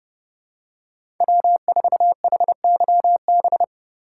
Morse code, heard as a keyed tone, about 30 wpm, 700 Hz. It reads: W4HYB